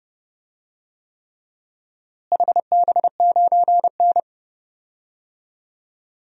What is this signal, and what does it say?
Morse code, heard as a keyed tone, about 30 wpm, 700 Hz. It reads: HB9N
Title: DXCC-100 Call Signs - 4 Characters - Repeated 3x - 30wpm